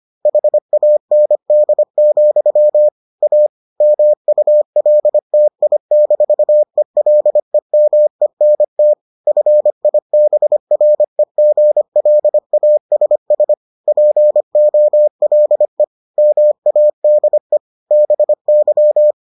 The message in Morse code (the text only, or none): HAND, A MULTI-ELEMENT FIBREGLASS POLE MADE BY